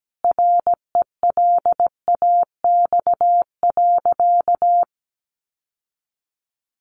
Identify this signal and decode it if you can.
Morse code, heard as a keyed tone, about 17 wpm, 700 Hz. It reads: RELAX.
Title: Sentences from Top 1400 Words 17wpm